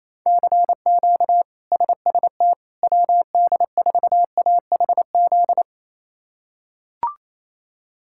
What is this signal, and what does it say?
Morse code, heard as a keyed tone, about 28 wpm, 700 Hz. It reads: CQ SST WD4AHZ
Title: K1USN SST QSO Practice - 28wpm